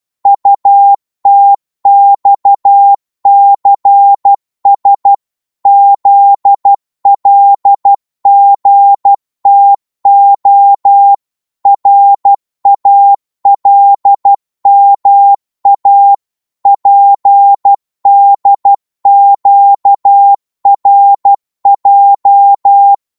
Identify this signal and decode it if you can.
Morse code, heard as a keyed tone, about 12 wpm, 800 Hz. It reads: UTXCS ZLGTO RALMA PDQRJ